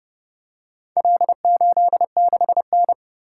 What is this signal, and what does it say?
Morse code, heard as a keyed tone, about 30 wpm, 700 Hz. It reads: L86N